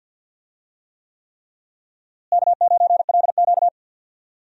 Morse code, heard as a keyed tone, about 52 wpm, 700 Hz. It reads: K9LQ
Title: USA Call Signs - 4 Characters - Repeated 3x - 50wpm